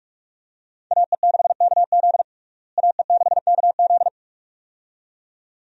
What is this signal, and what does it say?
Morse code, heard as a keyed tone, about 41 wpm, 700 Hz. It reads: AE6KZ AE6KZ
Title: USA Call Signs - 5 Characters - Repeated 2x - 45wpm